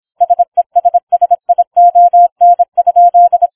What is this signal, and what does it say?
Morse code, heard as a keyed tone, about 26 wpm, 700 Hz. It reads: SESSION?